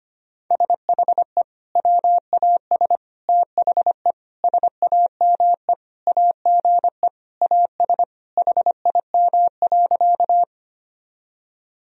Morse code, heard as a keyed tone, about 25 wpm, 700 Hz. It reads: SHE WAS THE SAME AGE AS HIM.
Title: Sentences from Top 1000 Words - Review Part 3 - 25wpm